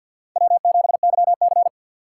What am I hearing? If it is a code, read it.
Morse code, heard as a keyed tone, about 49 wpm, 700 Hz. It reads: W7YC
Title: USA Call Signs - 4 Characters - 3x Character Spacing - 50wpm